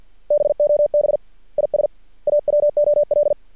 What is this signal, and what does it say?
Morse code, heard as a keyed tone, about 50 wpm, 590 Hz. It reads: BOB IS AWOL
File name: archived / CRSnet